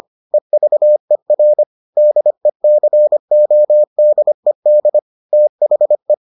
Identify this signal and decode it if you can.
Morse code, heard as a keyed tone, about 25 wpm, 600 Hz. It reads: EVER DECODED THE